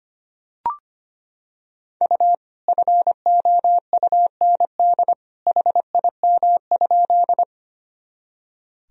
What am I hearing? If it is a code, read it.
Morse code, heard as a keyed tone, about 25 wpm, 700 Hz. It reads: U FOUND HIM?
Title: Sentences from Top 300 Words and Abbreviations 25wpm